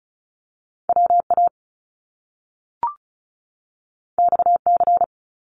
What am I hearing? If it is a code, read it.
Morse code, heard as a keyed tone, about 36 wpm, 700 Hz. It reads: WA XC
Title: Top Two Letter Combinations in Words 35wpm